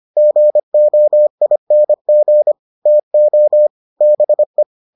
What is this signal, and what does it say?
Morse code, heard as a keyed tone, about 25 wpm, 600 Hz. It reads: GOING TO BE